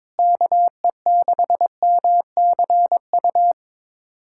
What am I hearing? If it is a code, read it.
Morse code, heard as a keyed tone, about 21 wpm, 700 Hz. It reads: KE6MCU
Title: USA Callsign DE Callsign 22wpm